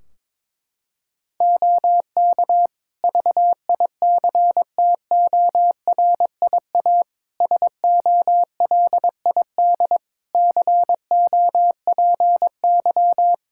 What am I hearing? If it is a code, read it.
Morse code, heard as a keyed tone, about 22 wpm, 700 Hz. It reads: OK VICTORIA SOLID COPY
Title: Practice QSO - Exchange 3 - More Info - 22wpm